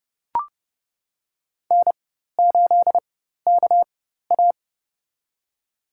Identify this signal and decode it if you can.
Morse code, heard as a keyed tone, about 30 wpm, spaced as for 14 wpm, 700 Hz. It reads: N8KA
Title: USA Call Signs - 4 Characters - 4x Character Spacing - 30wpm